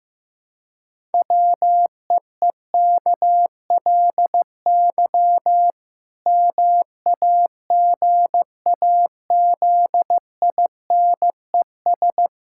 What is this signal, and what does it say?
Morse code, heard as a keyed tone, about 15 wpm, 700 Hz. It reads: WEEKLY MAGAZINES
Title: Pangram Challenge v2 - 15wpm